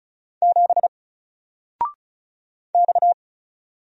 Morse code, heard as a keyed tone, about 36 wpm, 700 Hz. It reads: 7 X
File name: Single-Letter-Number-35wpm